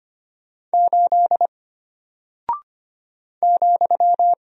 Morse code, heard as a keyed tone, about 25 wpm, 700 Hz. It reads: 8 ,
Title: Single Character (Letter:Number:Punctuation:Prosign) Alt 25wpm